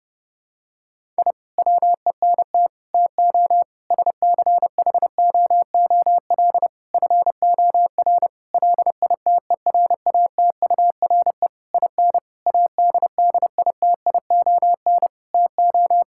Morse code, heard as a keyed tone, about 30 wpm, 700 Hz. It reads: I WENT TO SCHOOL FOR LITERATURE IN ADDITION TO
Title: Sentences from Top 1000 Words - Review Part 6 - 30wpm